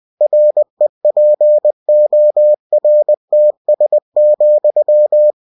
Morse code, heard as a keyed tone, about 20 wpm, 600 Hz. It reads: REPORTS,
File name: cw-books / winnie-the-pooh / winnie-20wpm-page-0112